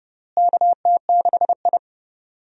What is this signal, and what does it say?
Morse code, heard as a keyed tone, about 30 wpm, 700 Hz. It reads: KT6I
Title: USA Call Signs - 4 Characters - Repeated 3x - 30wpm